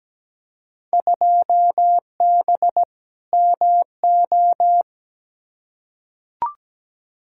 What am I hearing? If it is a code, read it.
Morse code, heard as a keyed tone, about 17 wpm, 700 Hz. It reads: 2B MO
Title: Field Day Exchanges - 17wpm